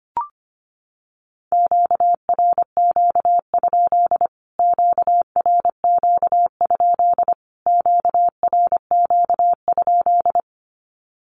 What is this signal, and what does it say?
Morse code, heard as a keyed tone, about 25 wpm, 700 Hz. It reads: QRQ? QRQ? QRQ?
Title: Q-Codes Repeated 3x 25wpm